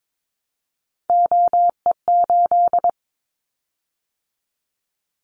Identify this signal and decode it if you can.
Morse code, heard as a keyed tone, about 22 wpm, 700 Hz. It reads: OE8
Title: Callsign Prefixes for Top 100 DXCC - 22wpm